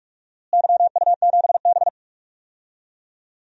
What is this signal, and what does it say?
Morse code, heard as a keyed tone, about 42 wpm, 700 Hz. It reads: YU7B
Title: DXCC-100 Call Signs - 4 Characters - Repeated 2x - 45wpm